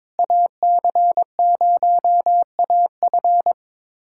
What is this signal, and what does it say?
Morse code, heard as a keyed tone, about 22 wpm, 700 Hz. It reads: AC0AF
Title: USA Callsign DE Callsign - 5 Characters - 22wpm